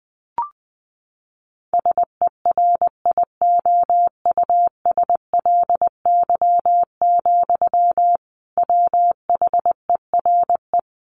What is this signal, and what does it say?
Morse code, heard as a keyed tone, about 20 wpm, 700 Hz. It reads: SERIOUSLY, WHERE
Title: Sentences from Top 1200 Words 20wpm